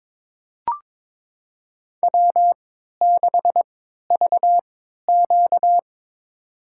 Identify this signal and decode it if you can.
Morse code, heard as a keyed tone, about 22 wpm, spaced as for 13 wpm, 700 Hz. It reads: W6VQ
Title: USA Call Signs - 4 Characters - 3x Character Spacing - 22wpm